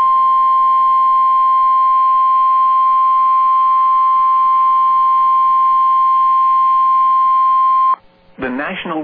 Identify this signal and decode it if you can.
Emergency Alert System (SAME) tones (the 1050 Hz attention tone), with speech. It is heard from 0.9 to 7.9 s.